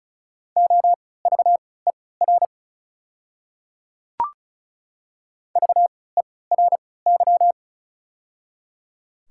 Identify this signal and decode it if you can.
Morse code, heard as a keyed tone, about 36 wpm, spaced as for 20 wpm, 700 Hz. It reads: OVER VERY
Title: 3x Character Spacing - Top 200 Words - 35wpm